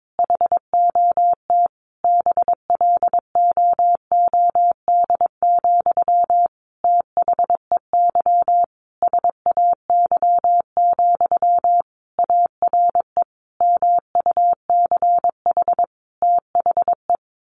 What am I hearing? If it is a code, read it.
Morse code, heard as a keyed tone, about 22 wpm, 700 Hz. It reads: HOT BLOOD, THEY SAY, ARE MUCH THE
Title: Sentences from Top 1100 Words 22wpm